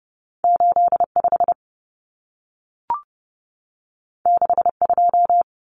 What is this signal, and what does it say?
Morse code, heard as a keyed tone, about 30 wpm, 700 Hz. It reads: 85 62
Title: Numbers - 2-Digit 30wpm